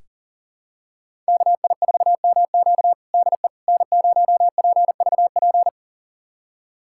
Morse code, heard as a keyed tone, about 40 wpm, 700 Hz. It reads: KI4MQ DE N0PVP
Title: USA Callsign DE Callsign - 5 Characters - 40wpm